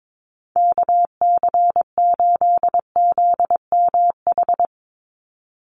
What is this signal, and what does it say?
Morse code, heard as a keyed tone, about 22 wpm, 700 Hz. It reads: KC8ZMH
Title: USA Call Signs 22wpm